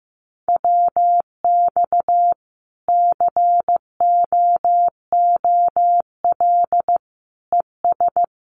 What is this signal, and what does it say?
Morse code, heard as a keyed tone, about 14 wpm, 700 Hz. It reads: WX COOL ES
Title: Common QSO Phrases 15wpm